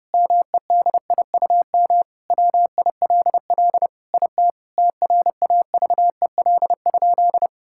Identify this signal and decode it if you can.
Morse code, heard as a keyed tone, about 30 wpm, 700 Hz. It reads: MEDIUM WILL IT TRAVEL?
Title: Sentences from Top 1000 Words - Review Part 6 - 30wpm